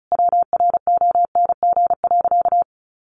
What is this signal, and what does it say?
Morse code, heard as a keyed tone, about 34 wpm, 700 Hz. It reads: WRONG.
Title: Sentences from Top 1200 Words 35wpm